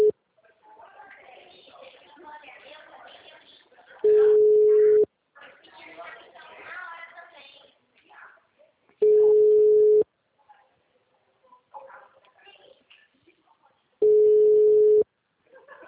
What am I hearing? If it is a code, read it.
Ringback tone (Europe, 425 Hz), with speech.